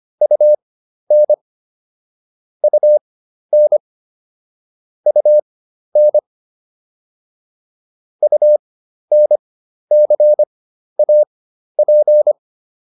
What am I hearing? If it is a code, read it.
Morse code, heard as a keyed tone, about 25 wpm, spaced as for 12 wpm, 600 Hz. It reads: UN UN UN UNCAP